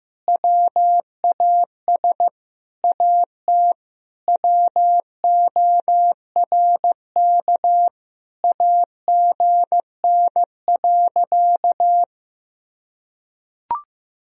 Morse code, heard as a keyed tone, about 15 wpm, 700 Hz. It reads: WAS AT WORK AGN.
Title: Sentences from Top 200 Words and Abbreviations 15wpm